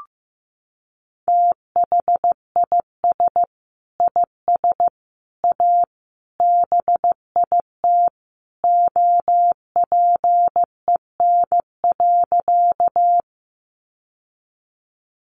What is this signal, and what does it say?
Morse code, heard as a keyed tone, about 15 wpm, 700 Hz. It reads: THIS IS A BIT OPEN.
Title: Sentences from Top 900 Words 15wpm